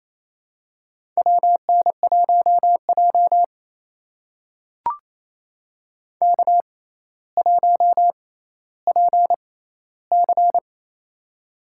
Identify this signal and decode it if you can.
Morse code, heard as a keyed tone, about 28 wpm, spaced as for 14 wpm, 700 Hz. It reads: WN1J K1PC
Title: USA Call Signs - 6x Character Spacing - 28wpm